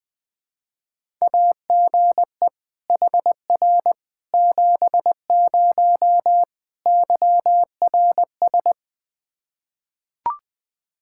Morse code, heard as a keyed tone, about 20 wpm, 700 Hz. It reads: AGE HR 70 YRS
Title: Practice QSO - Exchange 3 - More Info - 20wpm